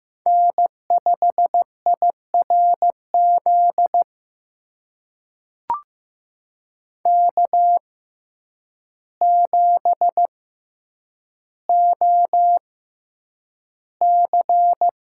Morse code, heard as a keyed tone, about 15 wpm, spaced as for 8 wpm, 700 Hz. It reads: N5IRZ K7OC
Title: USA Call Signs - 6x Character Spacing - 15wpm